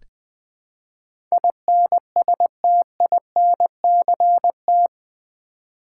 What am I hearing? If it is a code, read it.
Morse code, heard as a keyed tone, about 20 wpm, 700 Hz. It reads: INSTINCT
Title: Build A Word - 2 Letters At A Time - 8-Letter Words - 20wpm